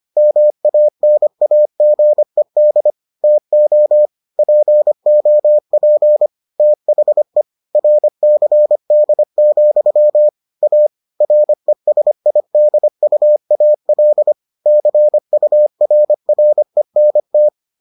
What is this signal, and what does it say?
Morse code, heard as a keyed tone, about 25 wpm, 600 Hz. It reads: MANAGED TO POP THE RCD, A RESIDUAL CURRENT